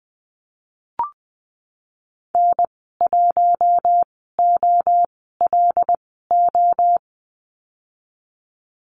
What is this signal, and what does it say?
Morse code, heard as a keyed tone, about 20 wpm, spaced as for 15 wpm, 700 Hz. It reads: N1OLO
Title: 186 Lesson 30g - Letter b - Call Signs 20wpm